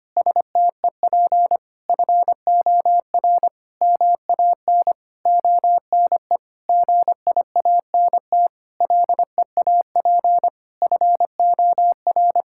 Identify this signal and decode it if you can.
Morse code, heard as a keyed tone, about 25 wpm, 700 Hz. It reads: STEP FOR MAN ONE GIANT LEAP FOR
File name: practice / 25wpm / Sentences with 9-Word Phrases - 25wpm